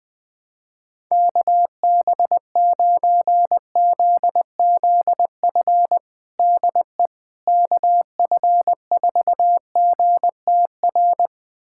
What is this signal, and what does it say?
Morse code, heard as a keyed tone, about 20 wpm, 700 Hz. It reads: KB9ZZF DE KF4GTR
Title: USA Callsign DE Callsign - 6 Characters - 20wpm